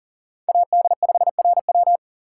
Morse code, heard as a keyed tone, about 40 wpm, 700 Hz. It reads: AD5RW
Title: USA Call Signs - 3x Character Spacing - 40wpm